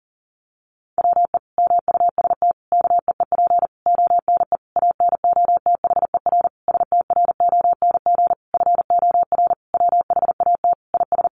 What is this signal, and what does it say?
Morse code, heard as a keyed tone, about 40 wpm, 700 Hz. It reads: WE MUST KEEP ONE ANOTHER STRONG FOR WHAT IS
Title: Sentences from Top 600 Words 40wpm